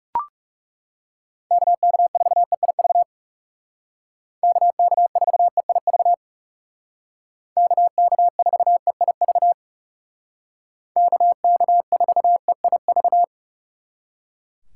Morse code, heard as a keyed tone, about 37 wpm, 700 Hz. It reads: KK4EIV KK4EIV KK4EIV KK4EIV
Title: Speed Racing - USA 2x3 Call Signs 30wpm